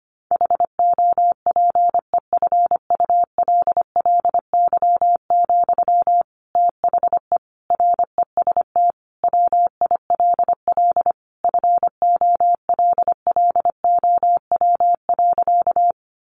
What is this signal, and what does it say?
Morse code, heard as a keyed tone, about 25 wpm, 700 Hz. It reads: HOPEFULLY, THE REST WILL FOLLOW.
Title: Sentences from Top 1500 Words 25wpm